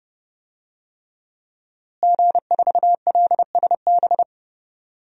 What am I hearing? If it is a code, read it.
Morse code, heard as a keyed tone, about 32 wpm, 700 Hz. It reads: G4LSB
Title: DXCC-100 Call Signs - 5 Characters - Repeated 3x - 30wpm